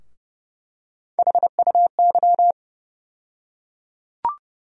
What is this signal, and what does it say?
Morse code, heard as a keyed tone, about 30 wpm, 700 Hz. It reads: HUY
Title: USA Call Sign Postfixes - 3 Letters 30wpm